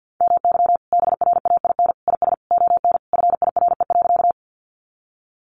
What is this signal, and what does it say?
Morse code, heard as a keyed tone, about 50 wpm, 700 Hz. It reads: MY BRAIN IS ON FIRE.
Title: Sentences from Top 1100 Words 50wpm